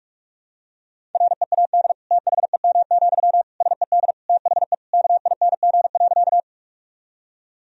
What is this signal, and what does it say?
Morse code, heard as a keyed tone, about 46 wpm, 700 Hz. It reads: READ THEM, SED THE KING.
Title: Sentences from Top 400 Words and Abbreviations 45wpm